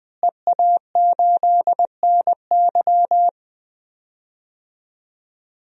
Morse code, heard as a keyed tone, about 20 wpm, 700 Hz. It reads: EA8NY
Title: DXCC-100 Call Signs - 5 Characters - Repeated 2x - 20wpm